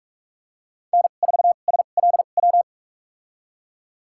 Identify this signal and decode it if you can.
Morse code, heard as a keyed tone, about 46 wpm, spaced as for 32 wpm, 690 Hz. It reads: N4SLW
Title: USA Call Signs - 5 Characters - 2x Character Spacing - 45wpm